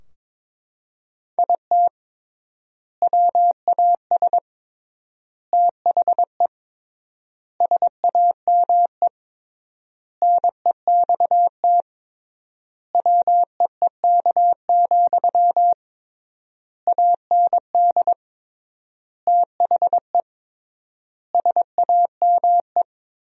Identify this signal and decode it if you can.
Morse code, heard as a keyed tone, about 22 wpm, 700 Hz. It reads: IT WAS THE SAME NEXT WEEK, AND THE SAME
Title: 3x Word-Spacing - Sentences from Top 500 Words - 22wpm